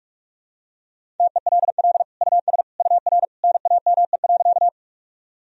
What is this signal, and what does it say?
Morse code, heard as a keyed tone, about 46 wpm, 700 Hz. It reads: TELL US UR NAME.
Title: Sentences from Top 200 Words and Abbreviations 45wpm